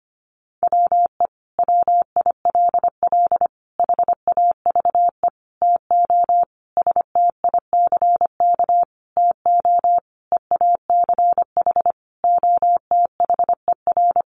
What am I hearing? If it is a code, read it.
Morse code, heard as a keyed tone, about 25 wpm, 700 Hz. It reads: WE WILL HAVE TO STICK TO EACH OTHER
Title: Sentences from Top 1200 Words 25wpm